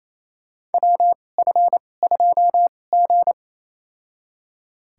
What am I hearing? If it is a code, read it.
Morse code, heard as a keyed tone, about 28 wpm, spaced as for 20 wpm, 700 Hz. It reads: WF2G
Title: USA Call Signs - 2x Character Spacing - 28wpm